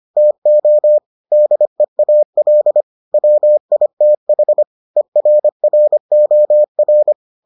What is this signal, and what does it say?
Morse code, heard as a keyed tone, about 25 wpm, 600 Hz. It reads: TO DEAL WITH ERROR